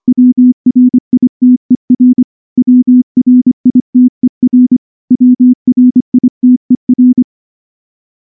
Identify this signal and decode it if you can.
Morse code, heard as a keyed tone, about 24 wpm, 250 Hz. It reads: WRITER WRITER WRITER
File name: morse-training-25WPM